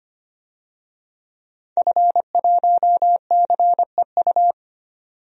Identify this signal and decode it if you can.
Morse code, heard as a keyed tone, about 25 wpm, 700 Hz. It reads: F1CEU
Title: DXCC-100 Call Signs - 5 Characters - 25wpm